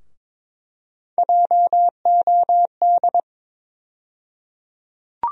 Morse code, heard as a keyed tone, about 21 wpm, 700 Hz. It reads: JOD
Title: USA Call Sign Postfixes - 3 Letters 22wpm